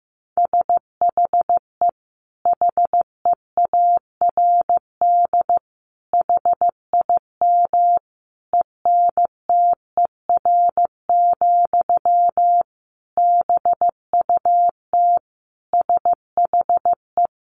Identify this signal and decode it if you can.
Morse code, heard as a keyed tone, about 15 wpm, 700 Hz. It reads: SHE HEARD HIM ENTER, BUT SHE
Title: Sentences from Top 1400 Words 15wpm